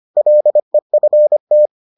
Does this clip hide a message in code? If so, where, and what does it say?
Yes: Morse code from 0.2 to 1.7 s, heard as a keyed tone, about 25 wpm, 600 Hz. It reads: LEFT